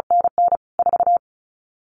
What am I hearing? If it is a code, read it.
Morse code, heard as a keyed tone, about 36 wpm, 700 Hz. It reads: NN 4
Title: CQ WW DX Contest Exchange Encore - 35wpm